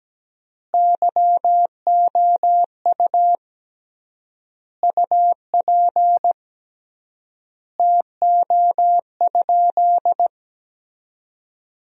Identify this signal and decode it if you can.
Morse code, heard as a keyed tone, about 17 wpm, 700 Hz. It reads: YOU UP TO?
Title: 3x Word-Spacing - Sentences from Top 100 Words - 17wpm